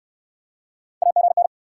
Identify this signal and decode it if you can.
Morse code, heard as a keyed tone, about 70 wpm, 700 Hz. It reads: AGN